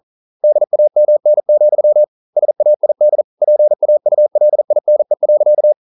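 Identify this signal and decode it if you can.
Morse code, heard as a keyed tone, about 40 wpm, 600 Hz. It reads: DAMN, SAID PAULINE.
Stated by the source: Morse code practice at 41 wpm, from CW med Gnister.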